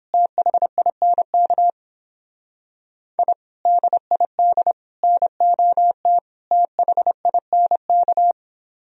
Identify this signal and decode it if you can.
Morse code, heard as a keyed tone, about 29 wpm, 700 Hz. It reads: THINK I DID NOT THINK